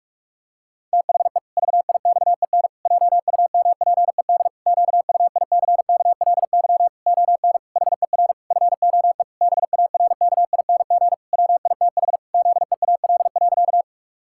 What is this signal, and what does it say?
Morse code, heard as a keyed tone, about 41 wpm, 700 Hz. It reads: THE VIXEN JUMPED QUICKLY ON HER FOE BARKING WITH ZEAL.
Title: Pangram Challenge v2 - 45wpm